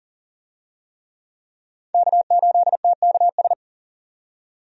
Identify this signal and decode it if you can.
Morse code, heard as a keyed tone, about 40 wpm, 700 Hz. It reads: K8TKS
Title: USA Call Signs - 5 Characters - 40wpm